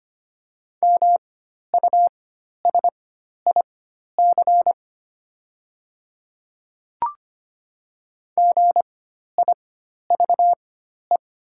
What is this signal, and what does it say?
Morse code, heard as a keyed tone, about 25 wpm, spaced as for 12 wpm, 700 Hz. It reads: MUSIC GIVE
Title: 4x Character Spacing - Top 300 Words - 25wpm